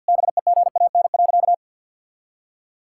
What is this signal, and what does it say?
Morse code, heard as a keyed tone, about 51 wpm, 700 Hz. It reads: BEGAN.
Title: Sentences from Top 1000 Words - Review Part 5 - 50wpm